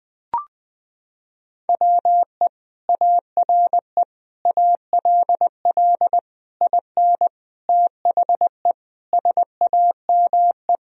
Morse code, heard as a keyed tone, about 20 wpm, 700 Hz. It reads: WE ARE ALL IN THE SAME
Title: Sentences from Top 1000 Words - Review Part 3 - 20wpm